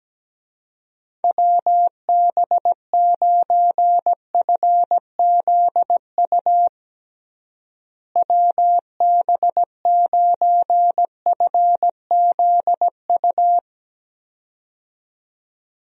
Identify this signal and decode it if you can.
Morse code, heard as a keyed tone, about 17 wpm, 700 Hz. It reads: WB9FZU WB9FZU
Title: USA Call Signs - 6 Characters - Repeated 2x - 17wpm